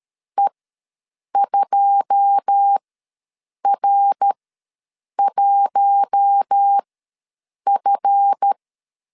Morse code, heard as a keyed tone, about 13 wpm, spaced as for 7 wpm, 800 Hz. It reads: E2R1F
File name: LEZIONI / 9 286 ws 12wpm 3351